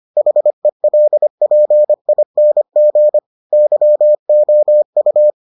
Morse code, heard as a keyed tone, about 25 wpm, 600 Hz. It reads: HELPING YOU